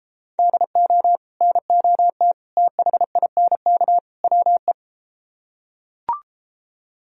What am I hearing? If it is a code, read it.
Morse code, heard as a keyed tone, about 33 wpm, 700 Hz. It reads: DO NOT THINK WE